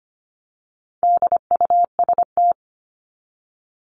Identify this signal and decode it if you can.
Morse code, heard as a keyed tone, about 25 wpm, 700 Hz. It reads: DUST
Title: Top 1751-2000 Words 25wpm